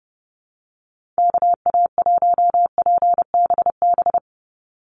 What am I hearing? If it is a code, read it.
Morse code, heard as a keyed tone, about 30 wpm, 700 Hz. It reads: KA1PBB
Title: USA Call Signs 30wpm